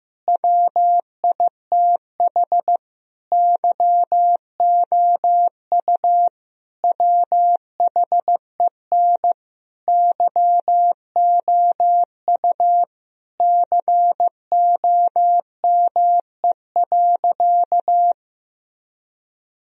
Morse code, heard as a keyed tone, about 15 wpm, 700 Hz. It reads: WITH YOU WHEN YOU COME.
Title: Sentences from Top 1000 Words - Review Part 4 - 15wpm